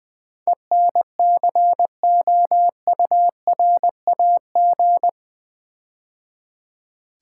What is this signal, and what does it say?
Morse code, heard as a keyed tone, about 20 wpm, 700 Hz. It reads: ENCOURAG
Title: Build A Word - 4 Letters At A Time - 11-Letter Words - 20wpm